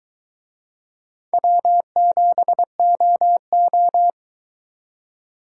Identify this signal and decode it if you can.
Morse code, heard as a keyed tone, about 23 wpm, 700 Hz. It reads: W7OO